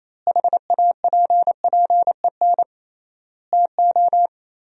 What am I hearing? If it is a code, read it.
Morse code, heard as a keyed tone, about 28 wpm, 700 Hz. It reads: HAPPEN TO
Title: 3x Word-Spacing - Sentences from Top 500 Words - 28wpm